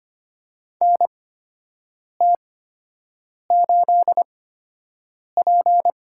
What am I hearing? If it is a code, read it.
Morse code, heard as a keyed tone, about 25 wpm, spaced as for 7 wpm, 700 Hz. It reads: NT8P
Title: USA Call Signs - 8x Character Spacing - 25wpm